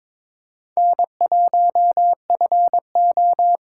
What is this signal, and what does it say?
Morse code, heard as a keyed tone, about 22 wpm, 700 Hz. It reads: N1FO
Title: USA Callsign DE Callsign - 4 Characters - 22wpm